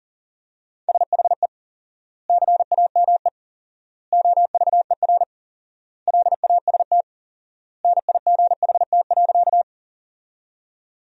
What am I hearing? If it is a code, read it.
Morse code, heard as a keyed tone, about 37 wpm, 700 Hz. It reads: SHE CAME OVER LAST NIGHT.
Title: 4x Word-Spacing - Sentences from Top 300 Words - 40wpm